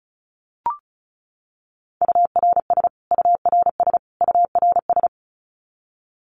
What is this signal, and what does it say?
Morse code, heard as a keyed tone, about 34 wpm, 700 Hz. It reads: URS URS URS
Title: Abbreviations Repeated 3x - 35wpm